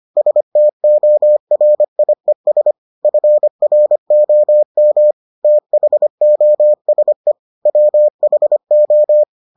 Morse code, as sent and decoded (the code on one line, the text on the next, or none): ... - --- .-. .. ... / ..-. .-. --- -- / - .... --- ... / .-- .... ---
STORIES FROM THOSE WHO